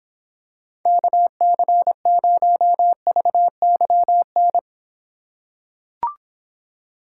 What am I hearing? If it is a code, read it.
Morse code, heard as a keyed tone, about 26 wpm, 700 Hz. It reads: KC0VYN